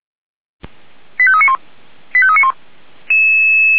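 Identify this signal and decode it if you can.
Five-tone selective call (ZVEI), heard as quick runs of five tones, tones of about 70 ms. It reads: 85391 85391
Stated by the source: ZVEI five-tone selective call received by SDR.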